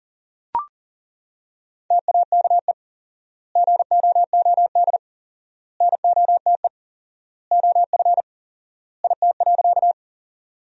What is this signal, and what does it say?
Morse code, heard as a keyed tone, about 40 wpm, 700 Hz. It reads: TAKE GOOD NOTE OF IT.
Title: 4x Word-Spacing - Sentences from Top 500 Words - 40wpm